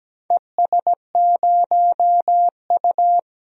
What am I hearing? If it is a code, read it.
Morse code, heard as a keyed tone, about 17 wpm, 700 Hz. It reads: ES0U
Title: DXCC-100 Call Signs - 4 Characters - Repeated 3x - 17wpm